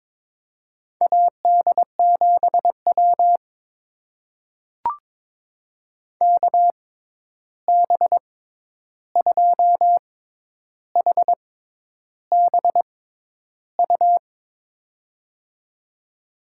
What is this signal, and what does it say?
Morse code, heard as a keyed tone, about 22 wpm, spaced as for 10 wpm, 700 Hz. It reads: AD7W KB2HBU
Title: USA Call Signs - 6x Character Spacing - 22wpm